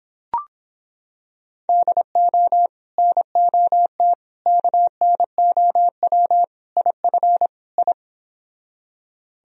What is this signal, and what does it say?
Morse code, heard as a keyed tone, about 26 wpm, 700 Hz. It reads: DO NOT KNOW IF I